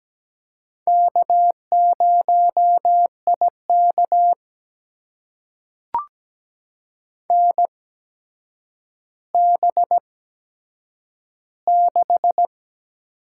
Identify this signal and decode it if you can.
Morse code, heard as a keyed tone, about 17 wpm, spaced as for 8 wpm, 700 Hz. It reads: K0IK NB6
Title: USA Call Signs - 4 Characters - 8x Character Spacing - 17wpm